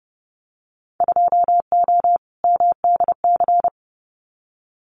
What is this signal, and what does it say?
Morse code, heard as a keyed tone, about 30 wpm, 700 Hz. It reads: 2O MDC
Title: Winter Field Day Exchanges 30wpm